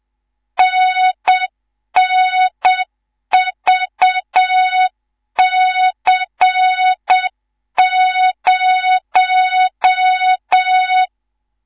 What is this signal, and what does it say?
Morse code, heard as a keyed tone, about 7 wpm, 3700 Hz, with noise. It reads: NNVC0
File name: Morze-HA2MN / hang / 19 GYAKORLAT